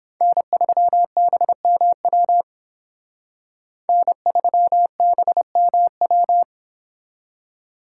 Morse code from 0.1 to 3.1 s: N3BMW

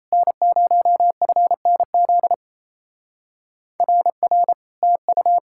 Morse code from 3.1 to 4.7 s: RR

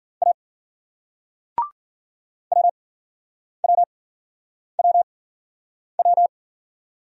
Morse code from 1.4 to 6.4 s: WWWW

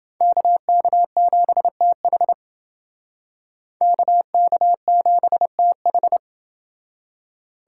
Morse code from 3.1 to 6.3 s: KK7TH

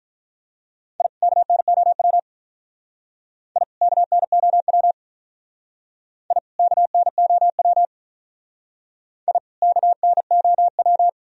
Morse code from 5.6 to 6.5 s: I

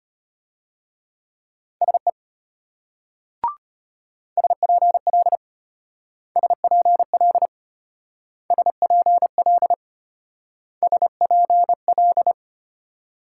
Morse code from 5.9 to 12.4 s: SPL SPL SPL

